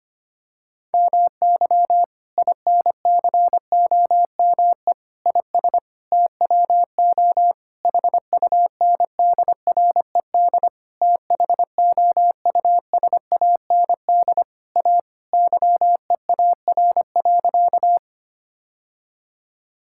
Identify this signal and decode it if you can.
Morse code, heard as a keyed tone, about 25 wpm, 700 Hz. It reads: MY INCOME IS TWO HUNDRED THOUSAND A YEAR.
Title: Sentences from Top 1000 Words - Review Part 5 - 25wpm